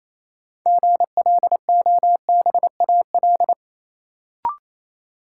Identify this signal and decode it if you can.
Morse code, heard as a keyed tone, about 28 wpm, 700 Hz. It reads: GLOBAL
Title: Sentences from Top 1200 Words - New Word Drill Alt - 28wpm